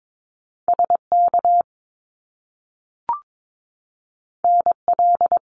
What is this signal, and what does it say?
Morse code, heard as a keyed tone, about 22 wpm, 700 Hz. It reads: SK NL
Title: Canadian Provinces and Territories 22wpm